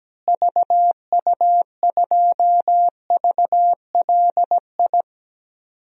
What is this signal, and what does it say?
Morse code, heard as a keyed tone, about 17 wpm, 700 Hz. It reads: VU2VLI
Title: DXCC-100 Call Signs - 6 Characters - Repeated 2x - 17wpm